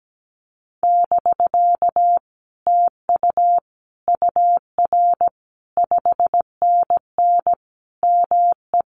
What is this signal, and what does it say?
Morse code, heard as a keyed tone, about 17 wpm, 700 Hz. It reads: <BK> TU UR 5NN ME
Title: Common QSO Phrases Encore 17wpm